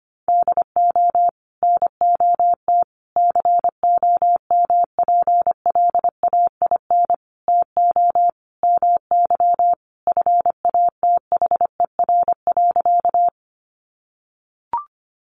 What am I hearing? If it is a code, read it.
Morse code, heard as a keyed tone, about 25 wpm, 700 Hz. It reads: DO NOT COMPLAIN TO MY FATHER.
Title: Sentences from Top 1300 Words 25wpm